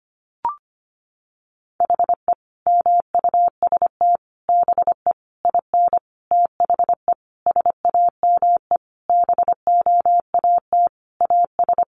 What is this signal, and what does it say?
Morse code, heard as a keyed tone, about 25 wpm, 700 Hz. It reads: HE MUST BE IN THE SAME BOAT AS
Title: Sentences from Top 500 Words 25wpm